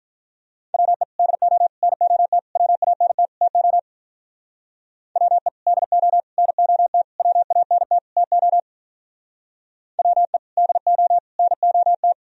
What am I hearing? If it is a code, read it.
Morse code, heard as a keyed tone, about 43 wpm, 700 Hz. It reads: WE DO NOT WANT TO WE DO NOT WANT TO WE DO NOT
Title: Speed-Racing - Sets of 5 Words 35wpm